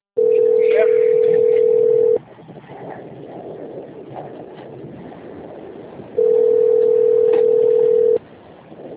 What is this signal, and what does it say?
Ringback tone (North America), 440 + 480 Hz.